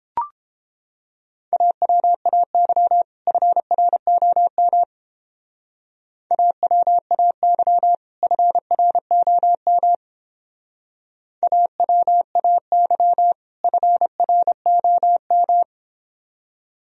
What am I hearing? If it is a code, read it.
Morse code, heard as a keyed tone, about 29 wpm, 700 Hz. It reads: AWAY FROM AWAY FROM AWAY FROM